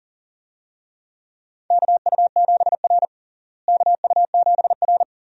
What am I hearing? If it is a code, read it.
Morse code, heard as a keyed tone, about 40 wpm, 700 Hz. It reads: KU7R KU7R
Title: USA Call Signs - 4 Characters - Repeated 2x - 40wpm